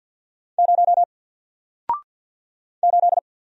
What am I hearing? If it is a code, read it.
Morse code, heard as a keyed tone, about 50 wpm, 700 Hz. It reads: 0 8
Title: Single Letter-Number Alt 50wpm